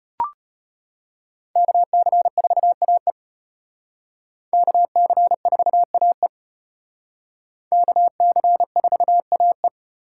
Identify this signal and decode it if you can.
Morse code, heard as a keyed tone, about 32 wpm, 700 Hz. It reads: KC4AE KC4AE KC4AE